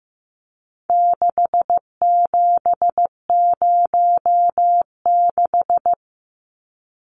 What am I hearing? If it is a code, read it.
Morse code, heard as a keyed tone, about 15 wpm, 700 Hz. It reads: 6706
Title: Numbers - 4-Digit 15wpm